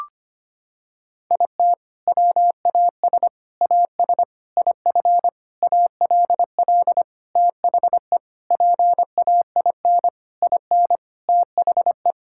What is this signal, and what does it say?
Morse code, heard as a keyed tone, about 25 wpm, 700 Hz. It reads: IT WAS AS IF ALL THE PAIN IN THE
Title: Sentences from Top 1600 Words 25wpm